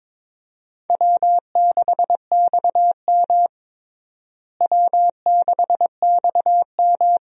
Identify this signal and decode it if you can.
Morse code, heard as a keyed tone, about 22 wpm, 700 Hz. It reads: W6XM W6XM
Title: USA Call Signs - 4 Characters - Repeated 2x - 22wpm